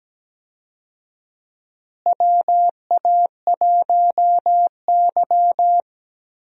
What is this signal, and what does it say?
Morse code, heard as a keyed tone, about 17 wpm, 700 Hz. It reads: WA1Y
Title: USA Call Signs - 4 Characters - Repeated 2x - 17wpm